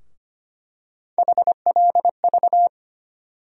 Morse code, heard as a keyed tone, about 25 wpm, 700 Hz. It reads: HLV
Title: USA Call Sign Postfixes - 3 Letters 25wpm